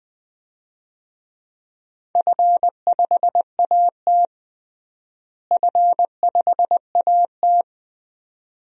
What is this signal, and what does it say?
Morse code, heard as a keyed tone, about 20 wpm, 700 Hz. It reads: F5AT F5AT
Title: DXCC-100 Call Signs - 4 Characters - Repeated 2x - 20wpm